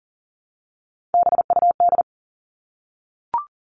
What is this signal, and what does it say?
Morse code, heard as a keyed tone, about 40 wpm, 700 Hz. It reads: BUD
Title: All Cut Numbers - 3-Digit 40wpm